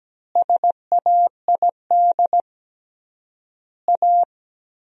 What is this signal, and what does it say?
Morse code, heard as a keyed tone, about 17 wpm, 700 Hz. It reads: SAID A
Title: 3x Word-Spacing - Sentences from Top 500 Words - 17wpm